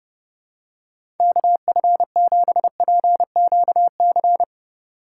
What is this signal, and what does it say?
Morse code, heard as a keyed tone, about 30 wpm, 700 Hz. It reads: KF7PQC